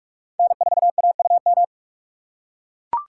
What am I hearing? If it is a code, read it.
Morse code, heard as a keyed tone, about 43 wpm, 700 Hz. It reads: N4AUM
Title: USA Call Signs - 5 Characters - 6x Character Spacing - 45wpm